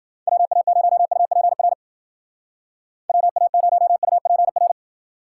Morse code, heard as a keyed tone, about 53 wpm, 700 Hz. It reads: WA0UPR WA0UPR